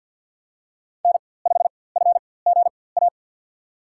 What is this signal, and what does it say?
Morse code, heard as a keyed tone, about 52 wpm, spaced as for 24 wpm, 700 Hz. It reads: N5FGA
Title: USA Call Signs - 5 Characters - 4x Character Spacing - 50wpm